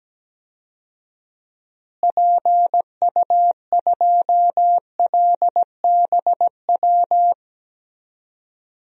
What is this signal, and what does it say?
Morse code, heard as a keyed tone, about 17 wpm, 700 Hz. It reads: PU2LBW
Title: DXCC-100 Call Signs - 6 Characters - Repeated 3x - 17wpm